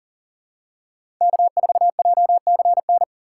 Morse code, heard as a keyed tone, about 37 wpm, 700 Hz. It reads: K4JCN
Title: USA Call Signs - 5 Characters - 6x Character Spacing - 40wpm